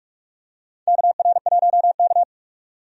Morse code, heard as a keyed tone, about 46 wpm, 700 Hz. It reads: KR1K
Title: USA Call Signs - 4 Characters - 45wpm